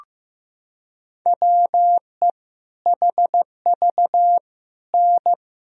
Morse code, heard as a keyed tone, about 15 wpm, 700 Hz. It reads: WE HV N
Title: Sentences from Top 100 Words and Abbreviations 15wpm